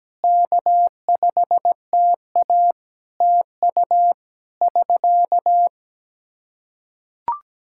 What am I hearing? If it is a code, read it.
Morse code, heard as a keyed tone, about 17 wpm, 700 Hz. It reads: K5TA TU <SK>
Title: K1USN SST QSO Practice 17wpm